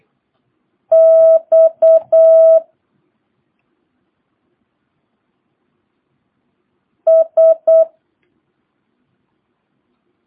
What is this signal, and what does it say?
Morse code, heard as a keyed tone, about 8 wpm, 650 Hz. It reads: X S
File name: jh8cbh / 2022cw